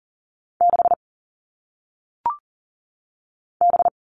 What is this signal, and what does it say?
Morse code, heard as a keyed tone, about 39 wpm, 700 Hz. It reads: 6 B
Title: All Cut Numbers - 1-Digit 40wpm